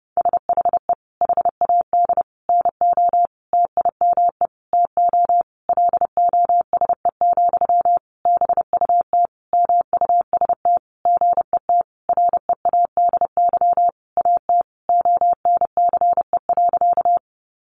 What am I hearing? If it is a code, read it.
Morse code, heard as a keyed tone, about 30 wpm, 700 Hz. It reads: SHE HAD NO TIME TO LOSE, BUT MUST GET READY AT ONCE.
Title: Sentences from Top 1100 Words 30wpm